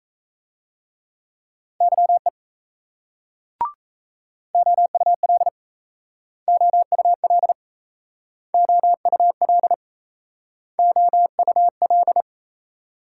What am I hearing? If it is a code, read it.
Morse code, heard as a keyed tone, about 38 wpm, 700 Hz. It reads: YE OUL OUL OUL OUL